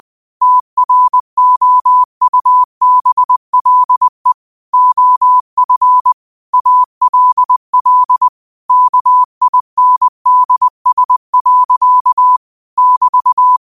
Morse code, heard as a keyed tone, about 20 wpm, 1000 Hz. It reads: TROUBLE OF ALL KINDS. <BT>
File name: qotdfiles / 20wpm